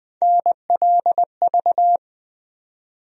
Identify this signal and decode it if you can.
Morse code, heard as a keyed tone, about 20 wpm, 700 Hz. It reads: NLV
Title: USA Call Sign Postfixes - 3 Letters 20wpm